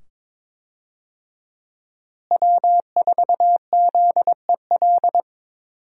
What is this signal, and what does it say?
Morse code, heard as a keyed tone, about 22 wpm, 700 Hz. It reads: W4ZEL
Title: USA Call Signs - 5 Characters - Repeated 2x - 22wpm